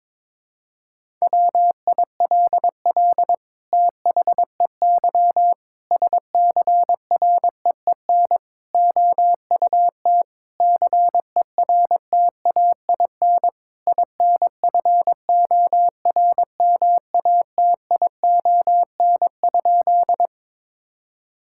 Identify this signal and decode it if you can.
Morse code, heard as a keyed tone, about 22 wpm, 700 Hz. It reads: WILL THEY SCREEN OUT CERTAIN INFORMATION?
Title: Sentences from Top 1000 Words - Review Part 6 - 22wpm